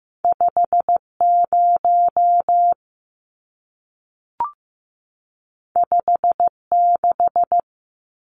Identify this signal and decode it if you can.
Morse code, heard as a keyed tone, about 15 wpm, 700 Hz. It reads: 50 56
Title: Numbers - 2-Digit 15wpm